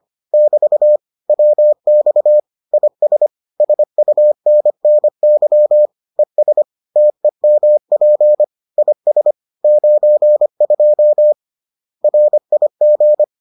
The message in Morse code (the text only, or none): <BT> WX IS SUNNY ES TEMP IS 92 RIG